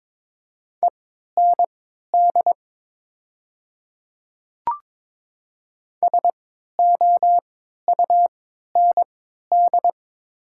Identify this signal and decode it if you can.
Morse code, heard as a keyed tone, about 22 wpm, spaced as for 13 wpm, 690 Hz. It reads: END SOUND
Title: 3x Character Spacing - Top 300 Words - 22wpm